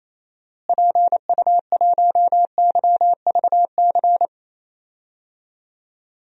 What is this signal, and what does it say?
Morse code, heard as a keyed tone, about 27 wpm, 700 Hz. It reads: PU1YVC
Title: DXCC-100 Call Signs - 6 Characters - Repeated 3x - 28wpm